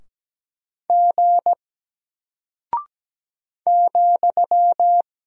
Morse code, heard as a keyed tone, about 17 wpm, 700 Hz. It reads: G ,
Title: Single Character (Letter:Number:Punctuation:Prosign) Alt 17wpm